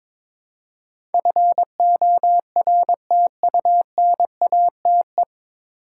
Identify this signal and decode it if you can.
Morse code, heard as a keyed tone, about 21 wpm, 700 Hz. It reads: FORTUNATE
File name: 9 Letter Words - First 3 Letters Given - 22wpm